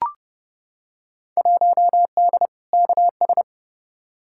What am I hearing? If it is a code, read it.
Morse code, heard as a keyed tone, about 30 wpm, 700 Hz. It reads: 1D KS
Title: Field Day Exchanges - 30wpm